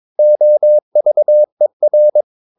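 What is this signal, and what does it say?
Morse code, heard as a keyed tone, about 22 wpm, 600 Hz. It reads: OVER
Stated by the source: Morse practice of short Danish words at 22 wpm, from CW med Gnister.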